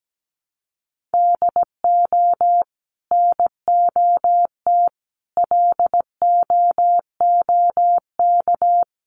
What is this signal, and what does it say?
Morse code, heard as a keyed tone, about 17 wpm, 700 Hz. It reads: DO NOT LOOK
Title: Sentences from Top 1100 Words 17wpm